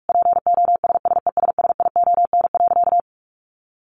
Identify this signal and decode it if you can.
Morse code, heard as a keyed tone, about 44 wpm, 700 Hz. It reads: POSSESSION.
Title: Sentences from Top 1400 Words 45wpm